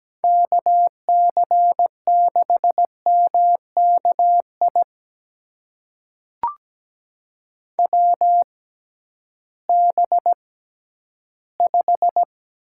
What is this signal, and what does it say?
Morse code, heard as a keyed tone, about 17 wpm, spaced as for 11 wpm, 700 Hz. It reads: KC6MKI WB5
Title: USA Call Signs - 6 Characters - 6x Character Spacing - 17wpm